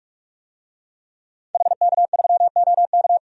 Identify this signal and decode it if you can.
Morse code, heard as a keyed tone, about 43 wpm, 700 Hz. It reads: HK3OK
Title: DXCC-100 Call Signs - 5 Characters - 45wpm